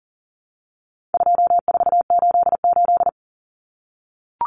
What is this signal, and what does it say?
Morse code, heard as a keyed tone, about 38 wpm, 700 Hz. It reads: 2488
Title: Common Cut Numbers - 4-Digit 40wpm